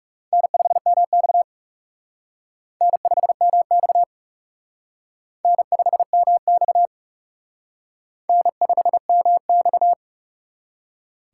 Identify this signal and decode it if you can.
Morse code, heard as a keyed tone, about 37 wpm, 700 Hz. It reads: N5MX N5MX N5MX N5MX
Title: Speed Racing - USA 1x2 Call Signs 30wpm